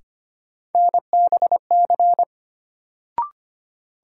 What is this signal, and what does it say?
Morse code, heard as a keyed tone, about 25 wpm, 700 Hz. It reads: NBC
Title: Sets of 5 Words Encore - New Word Drill - 25wpm